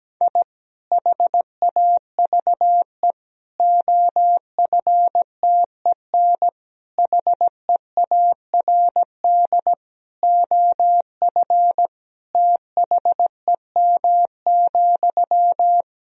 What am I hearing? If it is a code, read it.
Morse code, heard as a keyed tone, about 17 wpm, 700 Hz. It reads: I HAVE OFTEN HEARD OF THEM,
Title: Sentences from Top 1000 Words - Review Part 2 - 17wpm